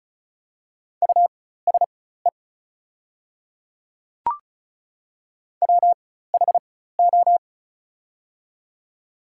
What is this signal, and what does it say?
Morse code, heard as a keyed tone, about 34 wpm, spaced as for 16 wpm, 700 Hz. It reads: USE WHO
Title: 4x Character Spacing - Top 100 Words - 35wpm